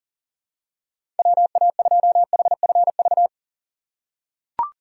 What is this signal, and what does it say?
Morse code, heard as a keyed tone, about 40 wpm, 700 Hz. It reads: WA2HFV
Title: USA Call Signs - 6 Characters - 2x Character Spacing - 40wpm